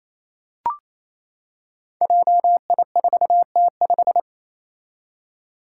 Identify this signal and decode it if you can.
Morse code, heard as a keyed tone, about 28 wpm, 700 Hz. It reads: JI4T5
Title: Five Challenging Letter-Number Combinations 28wpm